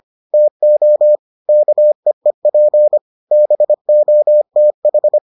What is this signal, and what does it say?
Morse code, heard as a keyed tone, about 25 wpm, 600 Hz. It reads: TO KEEP BOTH